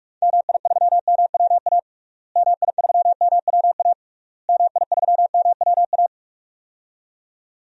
Morse code, heard as a keyed tone, about 46 wpm, 700 Hz. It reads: MI3MWA MI3MWA MI3MWA
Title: DXCC-100 Call Signs - 6 Characters - Repeated 3x - 45wpm